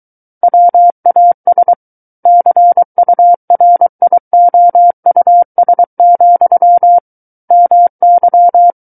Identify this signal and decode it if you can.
Morse code, heard as a keyed tone, about 23 wpm, 700 Hz. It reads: WAS CURIOUS, MY
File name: Book One / 14-Chapter 14 - 23 WPM